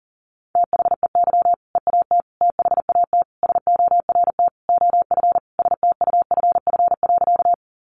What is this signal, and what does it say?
Morse code, heard as a keyed tone, about 40 wpm, 700 Hz. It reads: THEY EAT THAT SORT OF STUFF.
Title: Sentences from Top 1400 Words 40wpm